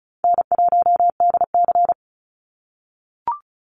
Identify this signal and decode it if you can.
Morse code, heard as a keyed tone, about 36 wpm, 700 Hz. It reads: N1DC
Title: Call Signs - 35wpm